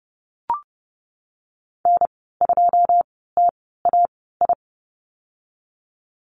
Morse code, heard as a keyed tone, about 30 wpm, spaced as for 17 wpm, 700 Hz. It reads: N2TAI
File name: Lesson 19g - Letter w - Call Signs 30wpm